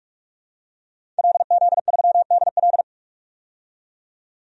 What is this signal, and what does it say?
Morse code, heard as a keyed tone, about 46 wpm, 700 Hz. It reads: LZ3DL
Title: DXCC-100 Call Signs - 5 Characters - 45wpm